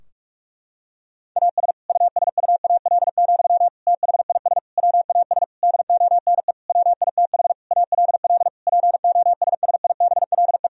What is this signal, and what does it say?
Morse code, heard as a keyed tone, about 46 wpm, 700 Hz. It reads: AS USUAL, THIS WAS DONE WITH ALL POSSIBLE
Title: Sentences from Top 1000 Words - Review Part 2 - 45wpm